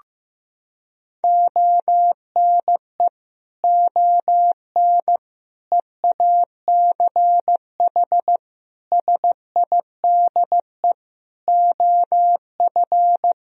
Morse code, heard as a keyed tone, about 15 wpm, 700 Hz. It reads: ONE ON EACH SIDE OF
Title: Sentences from Top 1000 Words - Review Part 2 - 15wpm